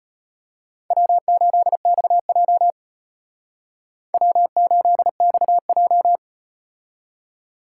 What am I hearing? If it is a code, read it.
Morse code, heard as a keyed tone, about 38 wpm, 700 Hz. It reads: W8XJ W8XJ